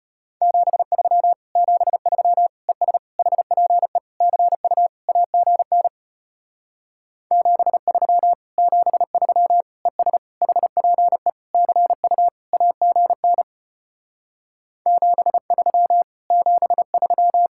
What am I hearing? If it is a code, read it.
Morse code, heard as a keyed tone, about 33 wpm, 700 Hz. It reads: 73 73 ES HPE CU AGN 73 73 ES HPE CU AGN 73 73